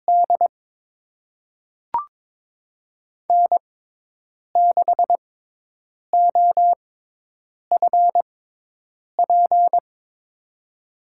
Morse code, heard as a keyed tone, about 22 wpm, spaced as for 8 wpm, 700 Hz. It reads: D N6OFP
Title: USA Call Signs - 5 Characters - 6x Character Spacing - 22wpm